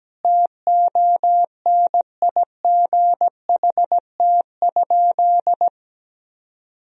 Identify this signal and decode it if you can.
Morse code, heard as a keyed tone, about 17 wpm, 700 Hz. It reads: TONIGHT?
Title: Sentences from Top 1000 Words - Review Part 3 - 17wpm